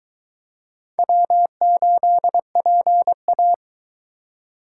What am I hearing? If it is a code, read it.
Morse code, heard as a keyed tone, about 21 wpm, 700 Hz. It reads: W8PA